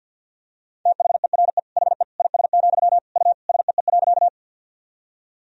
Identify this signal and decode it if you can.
Morse code, heard as a keyed tone, about 50 wpm, 700 Hz. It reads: THERE HE IS, U SEE.
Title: Sentences from Top 100 Words and Abbreviations 50wpm